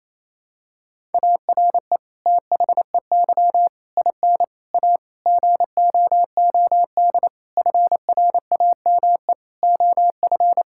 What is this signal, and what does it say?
Morse code, heard as a keyed tone, about 28 wpm, 700 Hz. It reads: ARE THEY IN A GOOD FRAME OF
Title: Sentences from Top 1000 Words - Review Part 6 - 28wpm